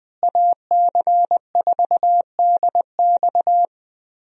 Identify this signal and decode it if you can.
Morse code, heard as a keyed tone, about 20 wpm, 700 Hz. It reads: AC4DX
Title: USA Call Signs - 5 Characters - Repeated 3x - 20wpm